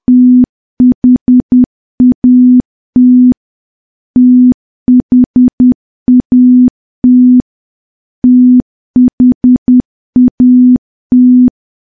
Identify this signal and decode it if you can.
Morse code, heard as a keyed tone, about 10 wpm, 250 Hz. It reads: THAT THAT THAT
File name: morse / morse-training-10WPM